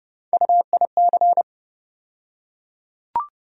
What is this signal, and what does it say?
Morse code, heard as a keyed tone, about 29 wpm, 700 Hz. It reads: UIC
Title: USA Call Sign Postfixes - 3 Letters 30wpm